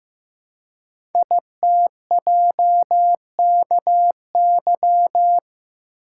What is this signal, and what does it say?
Morse code, heard as a keyed tone, about 15 wpm, 700 Hz. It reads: ITJKY
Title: Code Groups - 5 Letters - 15wpm